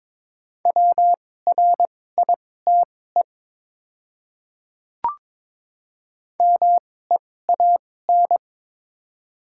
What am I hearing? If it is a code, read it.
Morse code, heard as a keyed tone, about 22 wpm, spaced as for 16 wpm, 690 Hz. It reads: WRITE MEAN
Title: 2x Character Spacing - Top 200 Words - 22wpm